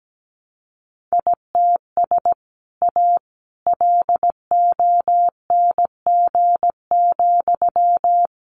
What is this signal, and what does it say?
Morse code, heard as a keyed tone, about 17 wpm, 700 Hz. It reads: ITS A LONG,
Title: Sentences from Top 600 Words 17wpm